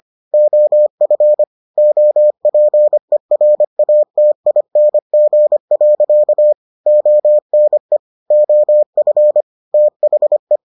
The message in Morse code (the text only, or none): OF OPERATING. ONE OF THE